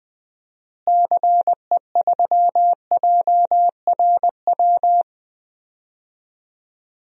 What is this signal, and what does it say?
Morse code, heard as a keyed tone, about 20 wpm, 700 Hz. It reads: CE3JRW
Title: DXCC-100 Call Signs - 6 Characters - Repeated 2x - 20wpm